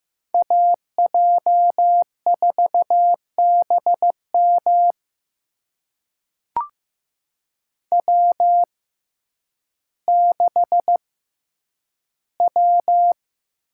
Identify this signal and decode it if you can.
Morse code, heard as a keyed tone, about 15 wpm, spaced as for 9 wpm, 700 Hz. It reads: AJ4BM W6W
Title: USA Call Signs - 5 Characters - 6x Character Spacing - 15wpm